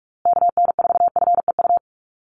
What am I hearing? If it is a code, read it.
Morse code, heard as a keyed tone, about 44 wpm, 700 Hz. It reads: KN4FEU
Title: USA Call Signs Repeated 2x 45wpm